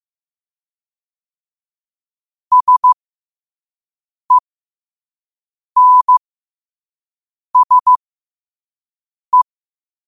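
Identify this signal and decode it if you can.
Morse code, heard as a keyed tone, about 15 wpm, spaced as for 5 wpm, 1000 Hz. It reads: SENSE
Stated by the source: Morse practice at 5 wpm.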